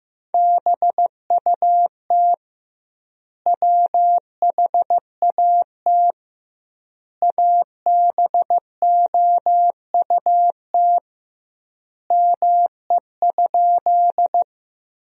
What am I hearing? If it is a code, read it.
Morse code, heard as a keyed tone, about 15 wpm, 700 Hz. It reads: BUT WHAT ABOUT ME?
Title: 2x Word-Spacing - Sentences from Top 200 Words - 15wpm